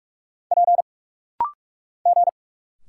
Morse code, heard as a keyed tone, about 46 wpm, 700 Hz. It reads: P G
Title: Single Letter-Number Alt - Rapid-Fire 45wpm